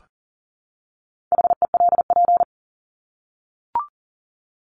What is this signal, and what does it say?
Morse code, heard as a keyed tone, about 40 wpm, 700 Hz. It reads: HELP
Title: Top 300 Words Repeated 40wpm